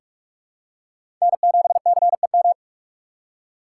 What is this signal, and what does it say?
Morse code, heard as a keyed tone, about 42 wpm, 700 Hz. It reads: N7CEM
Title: USA Call Signs - 5 Characters - 4x Character Spacing - 45wpm